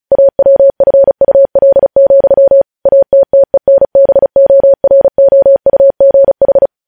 Morse code, heard as a keyed tone, about 36 wpm, 550 Hz. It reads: AWFUL, ATTENBOROUGH
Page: news headlines in morse code at 35 wpm